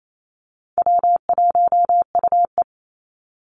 Morse code, heard as a keyed tone, about 28 wpm, 700 Hz. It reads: W1UE
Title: Easy Call Signs Repeated 2x - 28wpm